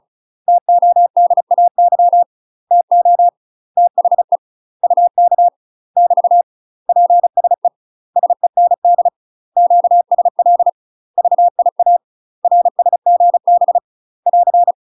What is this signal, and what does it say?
Morse code, heard as a keyed tone, about 34 wpm, 700 Hz. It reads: TODAY TO THE UK <BT> PSE SEND QSL VIA RSGB <AR>
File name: UK MASTER QSOs TRAINING / 35WPM / UK MASTER QSOs 0043